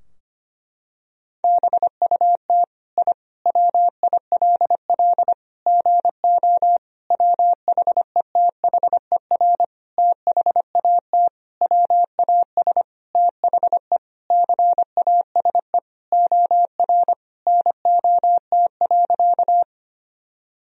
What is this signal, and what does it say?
Morse code, heard as a keyed tone, about 25 wpm, 700 Hz. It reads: BUT I WILL GO WHETHER THAT WAS THE CASE OR NOT.
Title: Sentences from Top 1000 Words - Review Part 3 - 25wpm